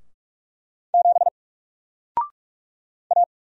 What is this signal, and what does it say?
Morse code, heard as a keyed tone, about 48 wpm, 700 Hz. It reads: 7 A
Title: Single Letter-Number Alt 45wpm